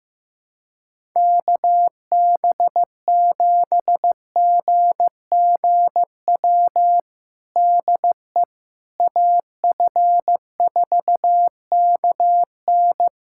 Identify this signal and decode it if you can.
Morse code, heard as a keyed tone, about 15 wpm, 700 Hz. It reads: KB7GGW DE AF4KN
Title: USA Callsign DE Callsign 15wpm